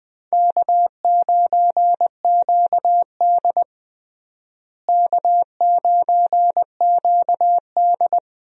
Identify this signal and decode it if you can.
Morse code, heard as a keyed tone, about 20 wpm, 700 Hz. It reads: K9QD K9QD
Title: USA Call Signs - 4 Characters - Repeated 2x - 20wpm